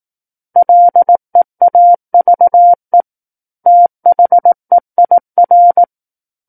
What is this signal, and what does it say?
Morse code, heard as a keyed tone, about 18 wpm, 700 Hz. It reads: LEAVE THEIR
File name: Book One / 09-Chapter 9 - 18 WPM